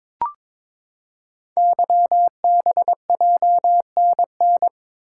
Morse code, heard as a keyed tone, about 22 wpm, 700 Hz. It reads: YBJNN
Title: Code Groups - 5 Letters - 22wpm